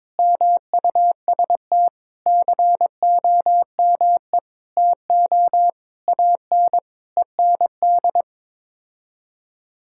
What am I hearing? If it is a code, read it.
Morse code, heard as a keyed tone, about 21 wpm, 700 Hz. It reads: MUST COME TO AN END
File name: practice / 22wpm / Sets of 8 Words - 22wpm